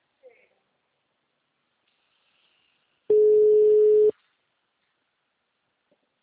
Ringback tone (Europe, 425 Hz).
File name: out- -2015- - - .46